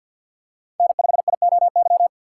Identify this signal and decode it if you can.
Morse code, heard as a keyed tone, about 51 wpm, 700 Hz. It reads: N5IOY
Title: USA Call Signs - 5 Characters - 3x Character Spacing - 50wpm